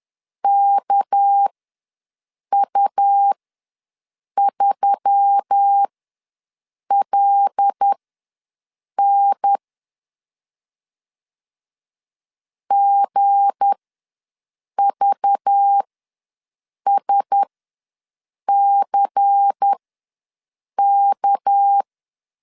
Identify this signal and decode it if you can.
Morse code, heard as a keyed tone, about 10 wpm, spaced as for 6 wpm, 800 Hz. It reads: KU3LN GVSCK
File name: LEZIONI / 7 GF9Z / 7 G ws 10wpm 3843